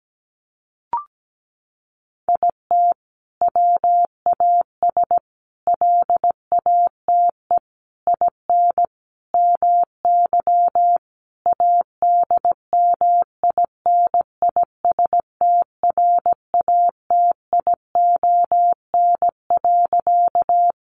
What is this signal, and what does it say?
Morse code, heard as a keyed tone, about 17 wpm, 700 Hz. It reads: IT WAS LATE IN MY ADMINISTRATION.
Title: Sentences from Top 1300 Words 17wpm